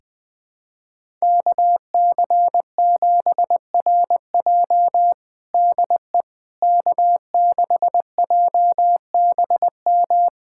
Morse code, heard as a keyed tone, about 20 wpm, 700 Hz. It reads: KC7RJ DE K6JBM
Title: USA Callsign DE Callsign - 5 Characters - 20wpm